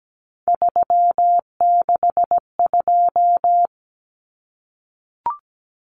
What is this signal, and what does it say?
Morse code, heard as a keyed tone, about 17 wpm, 700 Hz. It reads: 362
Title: Common Cut Numbers - 3-Digit 17wpm